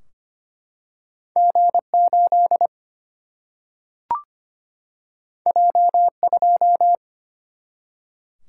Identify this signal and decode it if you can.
Morse code, heard as a keyed tone, about 25 wpm, 700 Hz. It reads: G 8 J 2
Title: Single-Letter Single-Number 25wpm